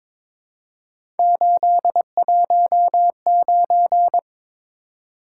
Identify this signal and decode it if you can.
Morse code, heard as a keyed tone, about 22 wpm, 700 Hz. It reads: 819
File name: practice / 22wpm / Three Letter-Number Combinations - 22wpm